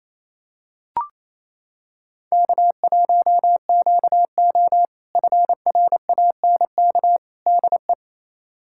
Morse code, heard as a keyed tone, about 28 wpm, 700 Hz. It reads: K1QO FRANK DE
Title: K1USN SST QSO Practice - 28wpm